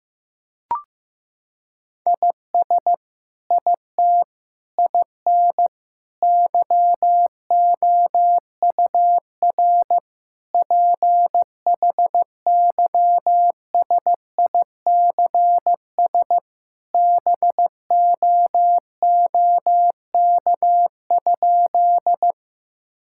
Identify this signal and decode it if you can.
Morse code, heard as a keyed tone, about 15 wpm, 700 Hz. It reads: IS IT IN YOUR PHYSICS BOOK?
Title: Sentences from Top 1000 Words - Review Part 5 - 15wpm